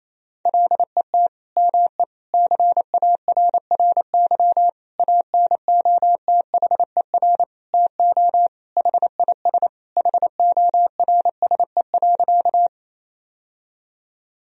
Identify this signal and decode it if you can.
Morse code, heard as a keyed tone, about 28 wpm, 700 Hz. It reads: LET ME CARRY ANOTHER TO HIS HORSE.
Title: Sentences from Top 1000 Words - Review Part 1 - 28wpm